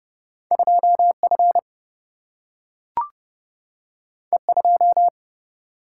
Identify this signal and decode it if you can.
Morse code, heard as a keyed tone, about 30 wpm, 700 Hz. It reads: 2 F E 2
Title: Single-Letter Single-Number 30wpm